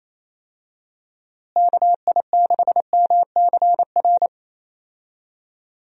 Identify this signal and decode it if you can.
Morse code, heard as a keyed tone, about 28 wpm, 700 Hz. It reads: KI6MCR
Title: USA Call Signs - 6 Characters - Repeated 2x - 28wpm